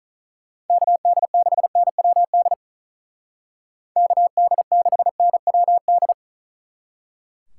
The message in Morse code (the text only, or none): KD6NWD KD6NWD